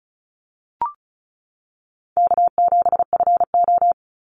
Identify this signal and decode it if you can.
Morse code, heard as a keyed tone, about 36 wpm, 700 Hz. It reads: K7FO
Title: NA SOTA Chaser Callsigns 35wpm